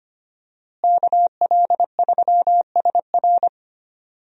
Morse code, heard as a keyed tone, about 25 wpm, 700 Hz. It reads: KL3SR
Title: USA Call Signs - 5 Characters - Repeated 3x - 25wpm